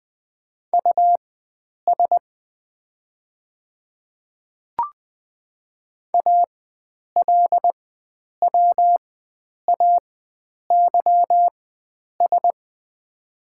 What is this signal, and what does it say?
Morse code, heard as a keyed tone, about 21 wpm, spaced as for 9 wpm, 700 Hz. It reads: US ALWAYS
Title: 4x Character Spacing - Top 300 Words - 20wpm